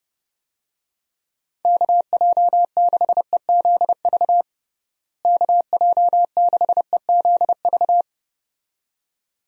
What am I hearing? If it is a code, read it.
Morse code, heard as a keyed tone, about 30 wpm, 700 Hz. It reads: KJ6EZV KJ6EZV
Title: USA Call Signs - 6 Characters - Repeated 2x - 30wpm